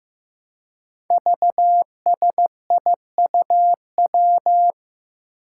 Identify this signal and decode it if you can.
Morse code, heard as a keyed tone, about 15 wpm, 700 Hz. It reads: VSIUW
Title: Five Challenging Letter-Number Combinations 15wpm